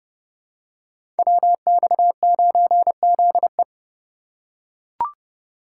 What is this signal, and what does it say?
Morse code, heard as a keyed tone, about 30 wpm, 700 Hz. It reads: WX9ZE
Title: Code Groups - 5 Letter:Numbers - 30wpm